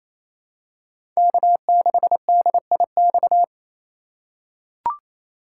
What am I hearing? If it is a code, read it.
Morse code, heard as a keyed tone, about 28 wpm, 700 Hz. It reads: K6DIX
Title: USA Call Signs - 5 Characters - 4x Character Spacing - 28wpm